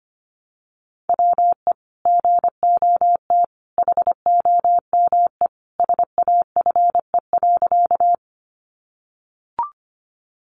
Morse code, heard as a keyed tone, about 25 wpm, 700 Hz. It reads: WE GOT HOME SAFE.
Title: Sentences from Top 900 Words 25wpm